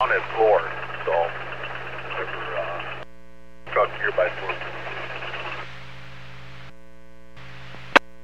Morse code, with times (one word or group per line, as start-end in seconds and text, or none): none